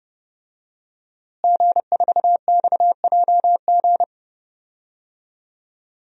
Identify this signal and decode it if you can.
Morse code, heard as a keyed tone, about 32 wpm, 700 Hz. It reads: G4XJG
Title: DXCC-100 Call Signs - 5 Characters - Repeated 3x - 30wpm